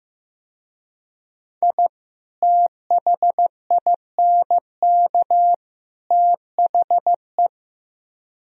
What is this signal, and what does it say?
Morse code, heard as a keyed tone, about 17 wpm, 700 Hz. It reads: I THINK THE
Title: Speed-Racing - Sets of 3 Words 15wpm